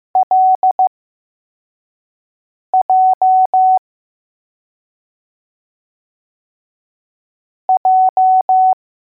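Morse code, heard as a keyed tone, about 15 wpm, spaced as for 4 wpm, 750 Hz. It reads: LJ J
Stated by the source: Morse code letters test drill for L, J and B.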